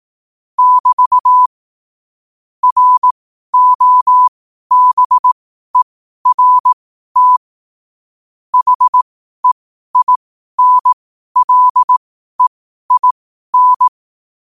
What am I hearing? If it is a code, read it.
Morse code, heard as a keyed tone, about 18 wpm, spaced as for 12 wpm, 1000 Hz. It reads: <BT> ROBERT HEINLEIN
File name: qotdfiles / 12wpm